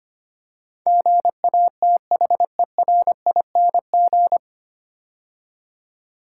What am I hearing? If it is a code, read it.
Morse code, heard as a keyed tone, about 25 wpm, 700 Hz. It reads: GATHERING
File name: practice / 25wpm / 9 Letter Words - First 3 Letters Given - 25wpm